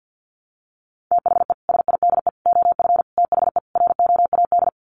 Morse code, heard as a keyed tone, about 50 wpm, 700 Hz. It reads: THE SIDE OF THE ROAD
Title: Sets of 5 Words 50wpm